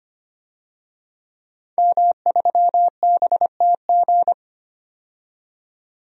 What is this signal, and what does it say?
Morse code, heard as a keyed tone, about 25 wpm, 700 Hz. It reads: M3BTG
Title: DXCC-100 Call Signs - 5 Characters - 25wpm